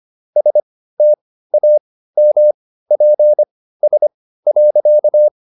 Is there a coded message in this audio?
Yes: Morse code from 0.4 to 5.3 s, heard as a keyed tone, about 25 wpm, spaced as for 15 wpm, 600 Hz. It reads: STAMPS.